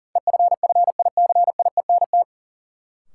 Morse code, heard as a keyed tone, about 40 wpm, 700 Hz. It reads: EFFICIENT
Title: Sentences from Top 1300 Words - New Word Drill Alt - 40wpm